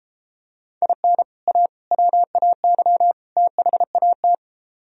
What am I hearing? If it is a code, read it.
Morse code, heard as a keyed tone, about 32 wpm, 700 Hz. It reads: IN A WAY THAT